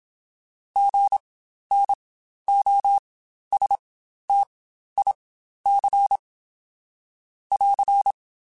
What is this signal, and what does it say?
Morse code, heard as a keyed tone, about 27 wpm, spaced as for 12 wpm, 800 Hz. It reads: GNOSTIC <AR>
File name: Spaced Alphas / Spaced Alphas mp3s / Gs sp @25